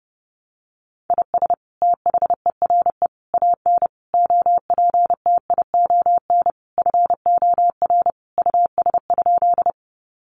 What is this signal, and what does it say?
Morse code, heard as a keyed tone, about 30 wpm, 700 Hz. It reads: IS THERE AN OPTION FOR US?
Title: Sentences from Top 1300 Words 30wpm